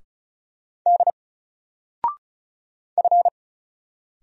Morse code, heard as a keyed tone, about 36 wpm, 700 Hz. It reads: D F
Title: Single Letter-Number 35wpm